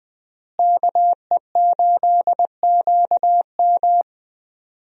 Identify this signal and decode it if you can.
Morse code, heard as a keyed tone, about 20 wpm, 700 Hz. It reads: KE8QM
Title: USA Call Signs - 5 Characters - Repeated 3x - 20wpm